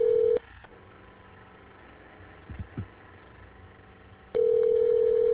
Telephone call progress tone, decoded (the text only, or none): ringback tone